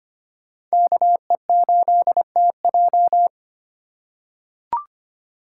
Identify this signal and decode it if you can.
Morse code, heard as a keyed tone, about 25 wpm, 700 Hz. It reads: KE8TJ
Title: USA Call Signs - 5 Characters - 3x Character Spacing - 25wpm